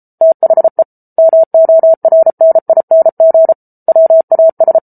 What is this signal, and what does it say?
Morse code, heard as a keyed tone, about 34 wpm, 650 Hz. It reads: THE MORNING WAS